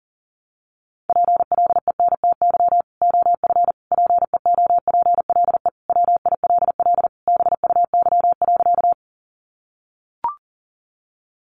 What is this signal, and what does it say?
Morse code, heard as a keyed tone, about 40 wpm, 700 Hz. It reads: PLENTY OF PEOPLE WILL BUY.
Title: Sentences from Top 1300 Words 40wpm